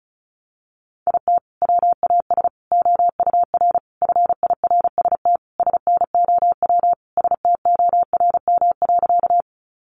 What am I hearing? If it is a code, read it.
Morse code, heard as a keyed tone, about 34 wpm, 700 Hz. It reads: IT WAS OUR FIRST SNOW STORM.
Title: Sentences from Top 1500 Words 35wpm